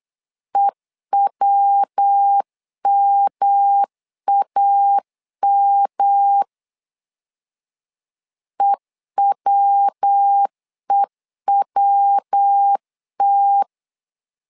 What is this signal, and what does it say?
Morse code, heard as a keyed tone, about 8 wpm, 800 Hz. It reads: EWMAM EWEWT
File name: LEZIONI / 3 MWRO / 3 MW 8wpm 2501